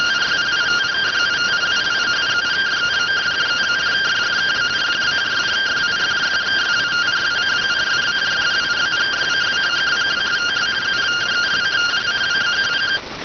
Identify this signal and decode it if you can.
Two-tone teleprinter signal, about 170 Hz shift.